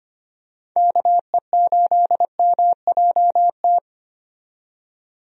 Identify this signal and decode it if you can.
Morse code, heard as a keyed tone, about 25 wpm, 700 Hz. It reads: KE8MJT
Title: USA Call Signs - 6 Characters - 25wpm